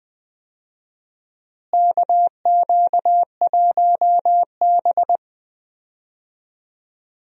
Morse code, heard as a keyed tone, about 20 wpm, 700 Hz. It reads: KQ1B
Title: USA Call Signs - 4 Characters - Repeated 2x - 20wpm